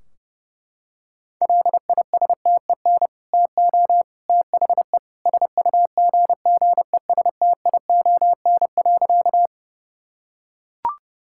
Morse code, heard as a keyed tone, about 32 wpm, 700 Hz. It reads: LISTEN TO THE SUGGESTION.
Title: Sentences from Top 1600 Words 30wpm